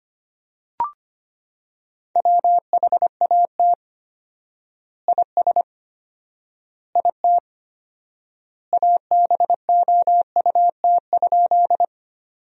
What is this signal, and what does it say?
Morse code, heard as a keyed tone, about 25 wpm, 700 Hz. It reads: WHAT IS IT ABOUT?
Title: 4x Word-Spacing - Sentences from Top 100 Words - 25wpm